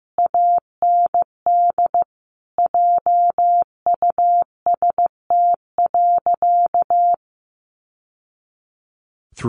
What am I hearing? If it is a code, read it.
Morse code, heard as a keyed tone, about 15 wpm, 700 Hz. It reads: AND JUST.
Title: Sentences from Top 800 Words 15wpm